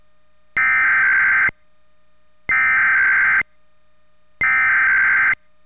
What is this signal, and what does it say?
Emergency Alert System (SAME) tones (header bursts), heard as harsh buzzing bursts. It is heard from 0.6 to 5.3 s.
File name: WHYY